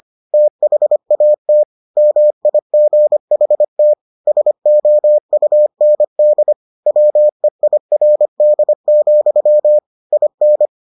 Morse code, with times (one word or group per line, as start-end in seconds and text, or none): 0.3-1.6 THAT
2.0-3.9 MIGHT
4.3-6.5 SOUND
6.9-9.8 WEIRD,
10.1-10.7 IN